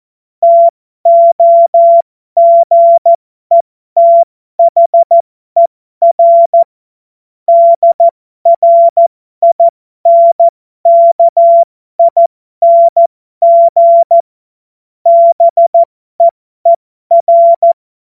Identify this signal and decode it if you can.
Morse code, heard as a keyed tone, about 14 wpm, spaced as for 12 wpm, 680 Hz. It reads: TOGETHER DRINKING BEER
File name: Morsekurs / Witze